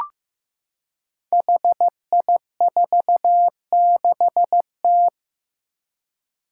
Morse code, heard as a keyed tone, about 15 wpm, 700 Hz. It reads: HI46T
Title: Five Challenging Letter-Number Combinations 15wpm